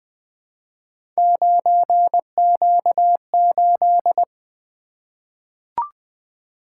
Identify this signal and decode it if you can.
Morse code, heard as a keyed tone, about 20 wpm, 700 Hz. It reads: 9Q8
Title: Three Letter-Number Combinations - 20wpm